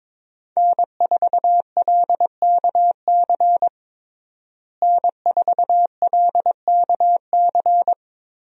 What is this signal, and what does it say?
Morse code, heard as a keyed tone, about 22 wpm, 700 Hz. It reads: N4LKC N4LKC
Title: USA Call Signs - 5 Characters - Repeated 2x - 22wpm